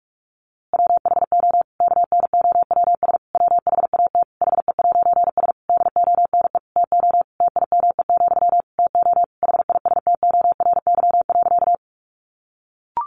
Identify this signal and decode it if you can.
Morse code, heard as a keyed tone, about 44 wpm, 700 Hz. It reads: WHO KNOWS WHAT HE'S DONE TO TIME, TO HISTORY.
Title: A Sound Of Thunder 45wpm